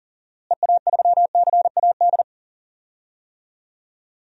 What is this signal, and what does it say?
Morse code, heard as a keyed tone, about 39 wpm, 700 Hz. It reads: EA3CAD
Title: DXCC-100 Call Signs - 6 Characters - Repeated 3x - 40wpm